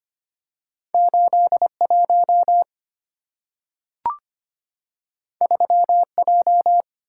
Morse code, heard as a keyed tone, about 25 wpm, 690 Hz. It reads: 81 3J
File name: Two Letter-Number Combinations - 25wpm